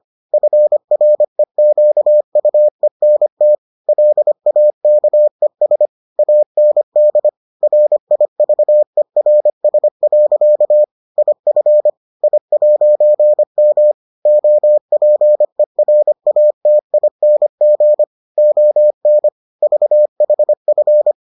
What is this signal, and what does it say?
Morse code, heard as a keyed tone, about 25 wpm, 600 Hz. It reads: FREQUENT LAKES AND RIVERS. IF I'M OPERATING ON VHF